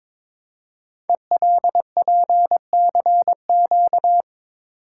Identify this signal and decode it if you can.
Morse code, heard as a keyed tone, about 22 wpm, 700 Hz. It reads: ELPCQ
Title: Code Groups - 5 Letters - 22wpm